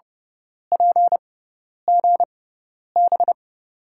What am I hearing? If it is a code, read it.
Morse code, heard as a keyed tone, about 29 wpm, spaced as for 10 wpm, 700 Hz. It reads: PGB
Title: USA Call Signs - 5 Characters - 6x Character Spacing - 30wpm